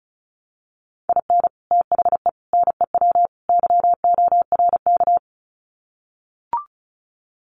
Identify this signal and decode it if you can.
Morse code, heard as a keyed tone, about 34 wpm, 700 Hz. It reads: IN THE NEW YORK
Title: Sets of 4 Words 35wpm